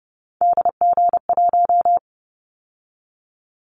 Morse code, heard as a keyed tone, about 30 wpm, 700 Hz. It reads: DG1
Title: Callsign Prefixes for Top 25 DXCC - 30wpm